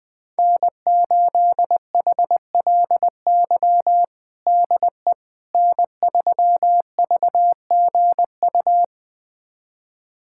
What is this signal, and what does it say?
Morse code, heard as a keyed tone, about 20 wpm, 700 Hz. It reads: N8HLY DE N3VGU
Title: USA Callsign DE Callsign - 5 Characters - 20wpm